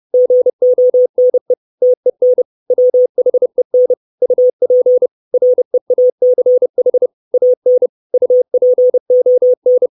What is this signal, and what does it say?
Morse code, heard as a keyed tone, about 30 wpm, 500 Hz. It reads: GONE TEN WHEN UP REACH AN UPON
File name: 632 Words 30WPM